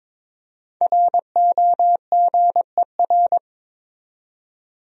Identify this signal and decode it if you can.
Morse code, heard as a keyed tone, about 22 wpm, 700 Hz. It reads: ROGER
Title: USA Common Names - Max 5 Letters - 22wpm